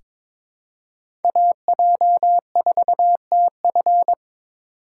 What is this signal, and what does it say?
Morse code, heard as a keyed tone, about 22 wpm, 700 Hz. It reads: AJ4TF
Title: K1USN SST QSO Practice - 22wpm